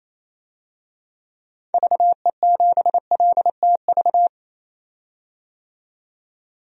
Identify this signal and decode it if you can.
Morse code, heard as a keyed tone, about 28 wpm, 700 Hz. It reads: VE7LTV
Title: DXCC-100 Call Signs - 6 Characters - 28wpm